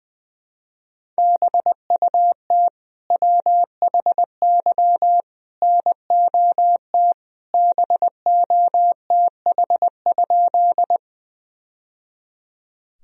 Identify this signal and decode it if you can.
Morse code, heard as a keyed tone, about 21 wpm, 700 Hz. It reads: BUT WHY NOT BOTH?
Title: Sentences from Top 1000 Words - Review Part 1 - 20wpm